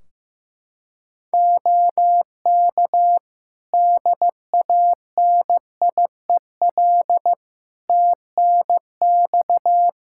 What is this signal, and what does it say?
Morse code, heard as a keyed tone, about 15 wpm, 700 Hz. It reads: OK DANIEL TNX
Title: K1USN SST QSO Practice 15wpm